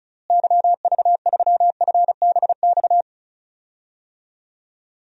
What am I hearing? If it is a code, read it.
Morse code, heard as a keyed tone, about 36 wpm, 700 Hz. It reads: YV3FBX
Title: DXCC-100 Call Signs - 6 Characters - Repeated 3x - 35wpm